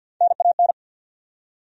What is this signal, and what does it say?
Morse code, heard as a keyed tone, about 51 wpm, 700 Hz. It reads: NAN
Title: USA Common Names - Max 3 Letters - 50wpm